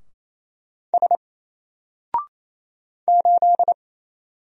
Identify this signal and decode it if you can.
Morse code, heard as a keyed tone, about 27 wpm, 700 Hz. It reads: S 8
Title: Single Character (Letter:Number:Punctuation:Prosign) 28wpm